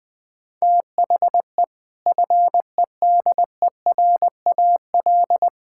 Morse code, heard as a keyed tone, about 20 wpm, 700 Hz. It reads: THE FEDERAL
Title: Sentences with 3-Word Phrases - 20wpm